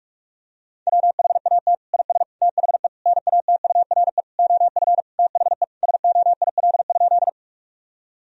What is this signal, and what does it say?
Morse code, heard as a keyed tone, about 46 wpm, 700 Hz. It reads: WHAT IS THE NATURE OF THE SOIL?
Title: Sentences from Top 1000 Words - Review Part 4 - 45wpm